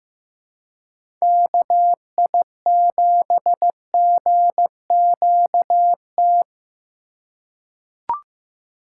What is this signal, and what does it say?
Morse code, heard as a keyed tone, about 15 wpm, 700 Hz. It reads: KI7GQT
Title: USA Call Signs - 6 Characters - 3x Character Spacing - 15wpm